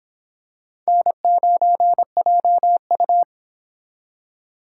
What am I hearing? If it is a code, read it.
Morse code, heard as a keyed tone, about 25 wpm, 700 Hz. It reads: N9JU